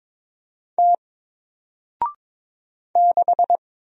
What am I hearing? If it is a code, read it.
Morse code, heard as a keyed tone, about 23 wpm, 700 Hz. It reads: T 6
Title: Single Letter-Number Alt 22wpm